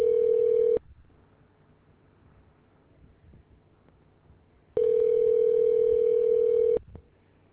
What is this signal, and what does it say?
Ringback tone (North America), 440 + 480 Hz.